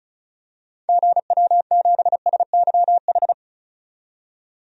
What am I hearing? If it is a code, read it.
Morse code, heard as a keyed tone, about 36 wpm, 700 Hz. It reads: GW7SYH
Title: DXCC-100 Call Signs - 6 Characters - 35wpm